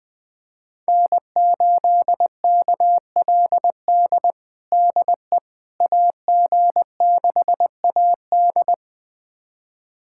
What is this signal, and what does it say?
Morse code, heard as a keyed tone, about 20 wpm, 700 Hz. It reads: N8KLD DE AG6AD
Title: USA Callsign DE Callsign - 5 Characters - 20wpm